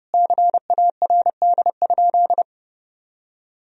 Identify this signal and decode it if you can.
Morse code, heard as a keyed tone, about 32 wpm, 700 Hz. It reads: CARD?
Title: Sentences from Top 1000 Words - Review Part 4 - 30wpm